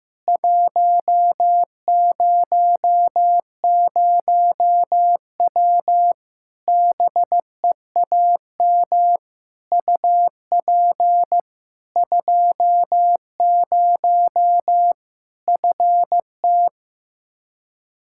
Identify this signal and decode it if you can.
Morse code, heard as a keyed tone, about 16 wpm, 700 Hz. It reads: 100W BEAM UP 20 FT
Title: K1USN SST QSO Practice 15wpm